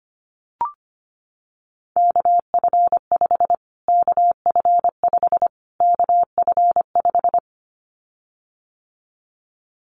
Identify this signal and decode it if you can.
Morse code, heard as a keyed tone, about 25 wpm, 700 Hz. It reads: KF5 KF5 KF5
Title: Callsign Prefixes for Top 25 DXCC - Repeated 3x - 25wpm